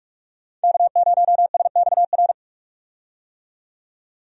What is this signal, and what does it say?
Morse code, heard as a keyed tone, about 46 wpm, 700 Hz. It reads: K0SXR
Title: USA Call Signs - 5 Characters - 45wpm